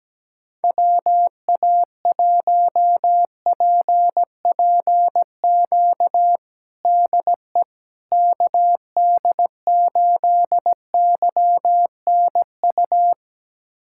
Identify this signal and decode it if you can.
Morse code, heard as a keyed tone, about 17 wpm, 700 Hz. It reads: WA1PPQ DE KD8YNU
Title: USA Callsign DE Callsign - 6 Characters - 17wpm